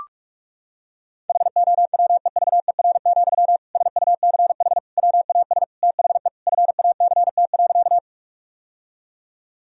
Morse code, heard as a keyed tone, about 41 wpm, 700 Hz. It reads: HOWEVER, SUCH WAS THE FACT.
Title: Sentences from Top 1000 Words - Review Part 3 - 45wpm